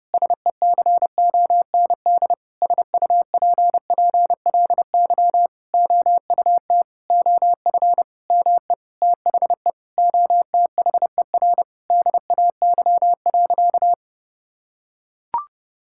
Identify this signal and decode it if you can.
Morse code, heard as a keyed tone, about 30 wpm, 700 Hz. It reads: SECOND SUPPLY OUT OF ME THE OTHER DAY.
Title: Sentences from Top 1000 Words - Review Part 5 - 30wpm